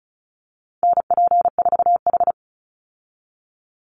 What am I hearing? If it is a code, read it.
Morse code, heard as a keyed tone, about 34 wpm, 700 Hz. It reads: NP4H
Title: Call Signs Repeated 2x - 35wpm